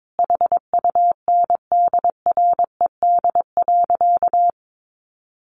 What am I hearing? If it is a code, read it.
Morse code, heard as a keyed tone, about 22 wpm, 700 Hz. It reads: HUNDRED.
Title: Sentences from Top 1300 Words 22wpm